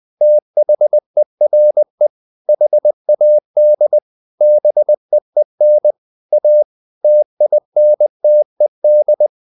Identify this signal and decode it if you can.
Morse code, heard as a keyed tone, about 20 wpm, 600 Hz. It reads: THERE HAD BEEN A TINTED
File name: the old man and the sea / the old man and the sea part 1 20wpm